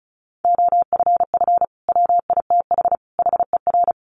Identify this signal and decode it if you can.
Morse code, heard as a keyed tone, about 34 wpm, 700 Hz. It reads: OFF WITH HER
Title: Sentences from Top 300 Words 35wpm